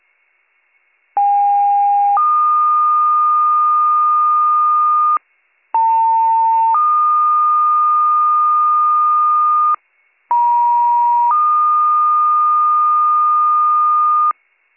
Two-tone page, heard as a one-second tone, then a three-second tone, about 810 then 1220 Hz, about 900 then 1220 Hz, about 950 then 1220 Hz. It is heard from 1.2 to 14.3 s.